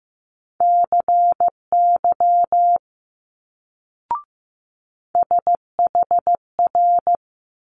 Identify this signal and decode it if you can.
Morse code, heard as a keyed tone, about 15 wpm, 700 Hz. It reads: CY SHR
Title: Multi-letter Phonemes Quick Fire 15wpm